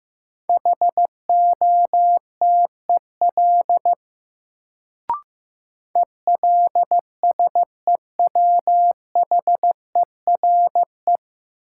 Morse code, heard as a keyed tone, about 15 wpm, 700 Hz. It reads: HOTEL ELSEWHERE
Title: Sentences from Top 1600 Words - New Word Drill - 15wpm